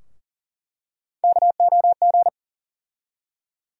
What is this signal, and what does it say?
Morse code, heard as a keyed tone, about 40 wpm, 700 Hz. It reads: KOG
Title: USA Call Sign Postfixes - 3 Letters 40wpm